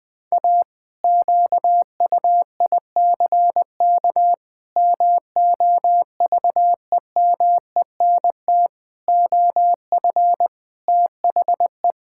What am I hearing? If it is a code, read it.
Morse code, heard as a keyed tone, about 20 wpm, 700 Hz. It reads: A QUICK MOVEMENT OF THE
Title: Pangram Challenge v2 - 20wpm